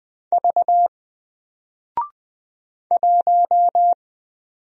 Morse code, heard as a keyed tone, about 20 wpm, 700 Hz. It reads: V 1
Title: Single Character (Letter:Number:Punctuation:Prosign) Alt 20wpm